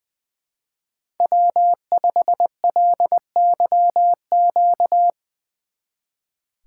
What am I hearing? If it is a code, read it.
Morse code, heard as a keyed tone, about 21 wpm, 700 Hz. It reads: W5LYQ
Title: Speed Racing - USA Call Signs 20wpm